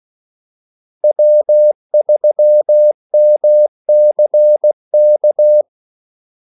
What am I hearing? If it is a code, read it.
Morse code, heard as a keyed tone, about 16 wpm, 600 Hz. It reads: W3MCK
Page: Kaldesignaler 16 wpm